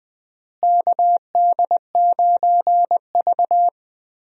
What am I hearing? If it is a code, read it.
Morse code, heard as a keyed tone, about 20 wpm, 700 Hz. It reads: KD9V
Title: USA Call Signs - 4 Characters - Repeated 2x - 20wpm